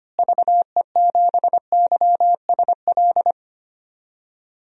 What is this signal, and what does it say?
Morse code, heard as a keyed tone, about 25 wpm, 700 Hz. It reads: VE7YSL
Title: DXCC-100 Call Signs - 6 Characters - Repeated 2x - 25wpm